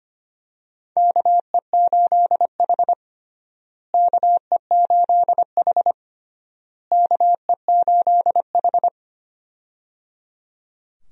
Morse code, heard as a keyed tone, about 25 wpm, 700 Hz. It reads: KE8H KE8H KE8H
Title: USA Call Signs - 4 Characters - Repeated 3x - 25wpm